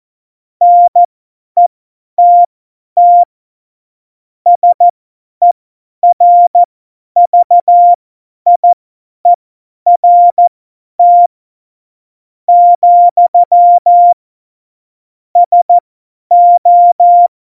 Morse code, heard as a keyed tone, about 14 wpm, spaced as for 9 wpm, 700 Hz. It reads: NETT SERVIERT, SO
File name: Morsekurs / Witze / Witz 26. Juni 2018, 50 F70